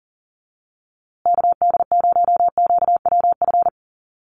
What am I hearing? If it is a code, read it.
Morse code, heard as a keyed tone, about 40 wpm, 700 Hz. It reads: KD0QWF
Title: USA Call Signs Repeated 3x 40wpm